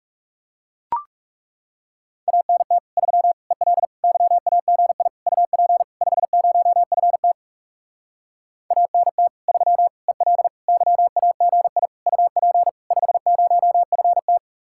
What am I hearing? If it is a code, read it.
Morse code, heard as a keyed tone, about 37 wpm, 700 Hz. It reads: ANT 3 EL YAGI UP 50FT ANT 3 EL YAGI UP 50FT
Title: Speed-Racing Common QSO Phrases - 30wpm